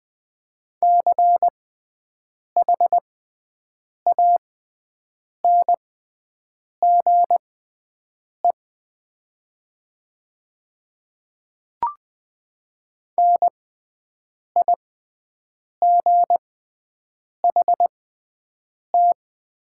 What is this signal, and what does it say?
Morse code, heard as a keyed tone, about 20 wpm, spaced as for 7 wpm, 700 Hz. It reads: CHANGE NIGHT
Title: 6x Character Spacing - Top 300 Words - 20wpm